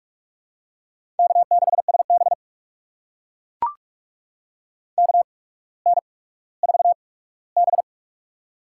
Morse code, heard as a keyed tone, about 44 wpm, spaced as for 19 wpm, 700 Hz. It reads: K6SB KN4B
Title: USA Call Signs - 4 Characters - 8x Character Spacing - 45wpm